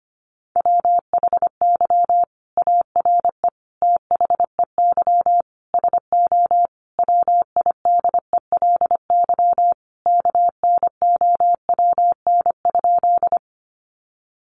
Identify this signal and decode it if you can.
Morse code, heard as a keyed tone, about 25 wpm, 700 Hz. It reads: WHY ARE THEY SO WIDELY KNOWN?
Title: Sentences from Top 1200 Words 25wpm